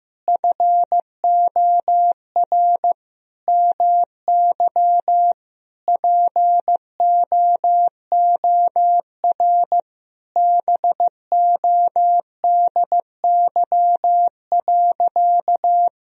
Morse code, heard as a keyed tone, about 16 wpm, 700 Hz. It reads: FOR MY POOR BODY.
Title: Sentences from Top 1000 Words - Review Part 2 - 15wpm